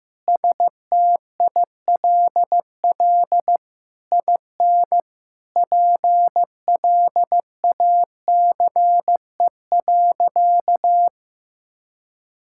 Morse code, heard as a keyed tone, about 15 wpm, 700 Hz. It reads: STILL IN PLACE.
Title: Sentences from Top 1000 Words - Review Part 6 - 15wpm